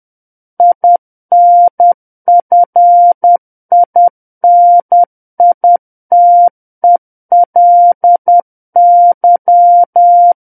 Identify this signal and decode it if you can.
Morse code, heard as a keyed tone, about 10 wpm, 700 Hz. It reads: INFINITELY
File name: Book One / 01-Chapter 1 - 10 WPM